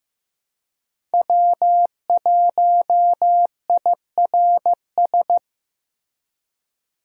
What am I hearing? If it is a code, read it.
Morse code, heard as a keyed tone, about 15 wpm, 700 Hz. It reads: W1IRS
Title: USA Call Signs - 5 Characters - Repeated 2x - 15wpm